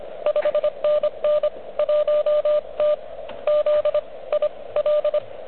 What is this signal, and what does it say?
Morse code, heard as a keyed tone, about 23 wpm, 590 Hz, with radio noise. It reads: 5NN 1T Z I L
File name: OK QRP 2022 / Zvuk / REC36